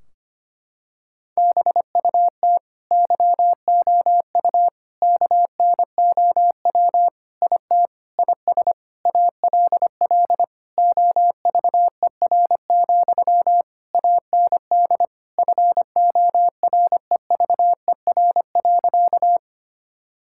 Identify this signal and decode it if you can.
Morse code, heard as a keyed tone, about 25 wpm, 700 Hz. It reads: BUT YOU KNOW IT IS ALL OVER, AND FOREVER.
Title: Sentences from Top 1600 Words 25wpm